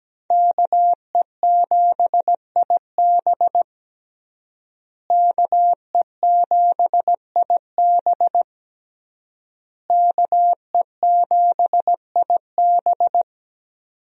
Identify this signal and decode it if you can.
Morse code, heard as a keyed tone, about 17 wpm, 700 Hz. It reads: KE7IB KE7IB KE7IB
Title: USA Call Signs - 5 Characters - Repeated 3x - 17wpm